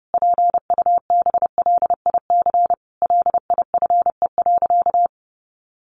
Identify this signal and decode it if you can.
Morse code, heard as a keyed tone, about 30 wpm, 700 Hz. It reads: PUBLIC LIFE.
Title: Sentences from Top 600 Words 30wpm